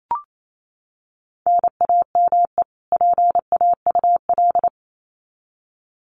Morse code, heard as a keyed tone, about 28 wpm, 700 Hz. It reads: NAME PAUL
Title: Common QSO Phrases 28wpm